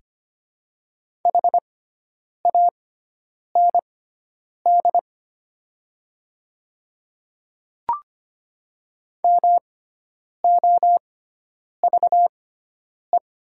Morse code, heard as a keyed tone, about 25 wpm, spaced as for 9 wpm, 700 Hz. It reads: HAND MOVE
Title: 6x Character Spacing - Top 200 Words - 25wpm